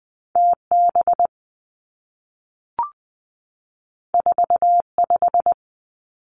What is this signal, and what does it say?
Morse code, heard as a keyed tone, about 20 wpm, 700 Hz. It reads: T B 4 5
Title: Two Challenging Letter-Number Combinations 20wpm